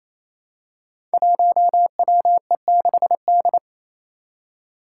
Morse code, heard as a keyed tone, about 28 wpm, 700 Hz. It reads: 1WE6D
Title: Five Challenging Letter-Number Combinations 28wpm